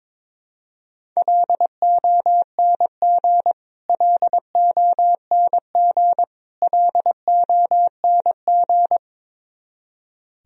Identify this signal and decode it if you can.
Morse code, heard as a keyed tone, about 22 wpm, 700 Hz. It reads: LONG LONG LONG
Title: Top 100 Words Repeated 22wpm